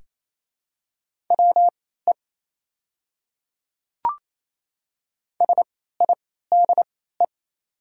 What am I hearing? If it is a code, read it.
Morse code, heard as a keyed tone, about 28 wpm, spaced as for 16 wpm, 700 Hz. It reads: WE SIDE
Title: 3x Character Spacing - Top 100 Words - 28wpm